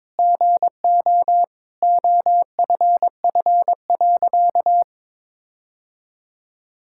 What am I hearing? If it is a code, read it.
Morse code, heard as a keyed tone, about 22 wpm, 700 Hz. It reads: GO OFF.
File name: practice / 22wpm / Sentences with 4-Word Phrases - 22wpm